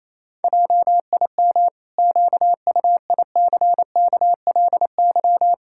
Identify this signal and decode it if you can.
Morse code, heard as a keyed tone, about 28 wpm, 700 Hz. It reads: JIM QUICKLY
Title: Pangram Challenge v2 - 28wpm